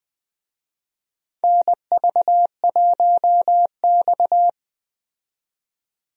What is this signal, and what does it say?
Morse code, heard as a keyed tone, about 20 wpm, 700 Hz. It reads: NV1X
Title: USA Call Signs - 4 Characters - 20wpm